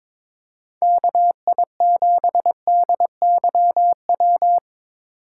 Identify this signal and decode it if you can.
Morse code, heard as a keyed tone, about 23 wpm, 700 Hz. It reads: KI7DYW
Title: USA Call Signs - 6 Characters - 4x Character Spacing - 22wpm